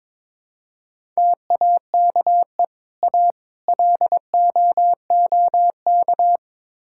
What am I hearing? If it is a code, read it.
Morse code, heard as a keyed tone, about 22 wpm, 700 Hz. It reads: TAKE A LOOK
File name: practice / 22wpm / Build A Phrase - 5-Word Phrases - 22wpm